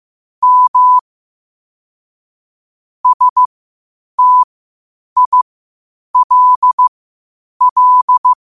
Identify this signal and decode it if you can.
Morse code, heard as a keyed tone, about 15 wpm, spaced as for 8 wpm, 1000 Hz. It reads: M STILL